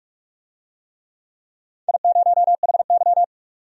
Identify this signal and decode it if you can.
Morse code, heard as a keyed tone, about 46 wpm, 700 Hz. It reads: I0HY
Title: DXCC-100 Call Signs - 4 Characters - 45wpm